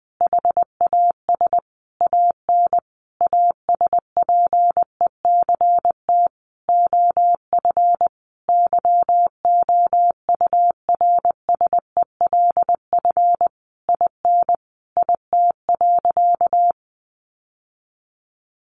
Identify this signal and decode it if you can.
Morse code, heard as a keyed tone, about 20 wpm, 700 Hz. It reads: HAS AN ASPECT OF YOURSELF IN IT.
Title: Sentences from Top 1300 Words 20wpm